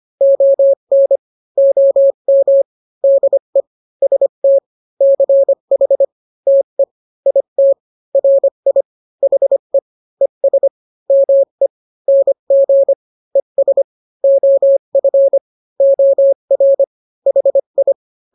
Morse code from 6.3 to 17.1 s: TE IT RI HE ES ME NG ES OF OR